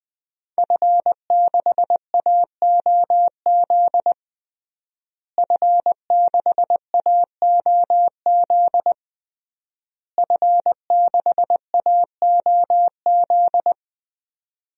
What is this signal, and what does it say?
Morse code, heard as a keyed tone, about 20 wpm, 700 Hz. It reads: F6AOZ F6AOZ F6AOZ
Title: DXCC-100 Call Signs - 5 Characters - Repeated 3x - 20wpm